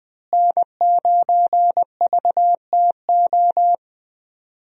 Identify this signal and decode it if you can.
Morse code, heard as a keyed tone, about 20 wpm, 700 Hz. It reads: N9VTO
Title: USA Call Signs - 5 Characters - Repeated 3x - 20wpm